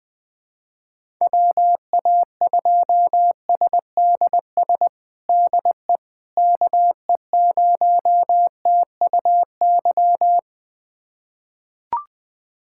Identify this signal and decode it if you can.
Morse code, heard as a keyed tone, about 21 wpm, 700 Hz. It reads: WA2SDS DE KE0TUY
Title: USA Callsign DE Callsign - 6 Characters - 20wpm